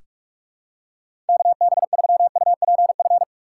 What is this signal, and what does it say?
Morse code, heard as a keyed tone, about 46 wpm, 700 Hz. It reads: KB3UPF
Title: USA Call Signs - 6 Characters - 3x Character Spacing - 45wpm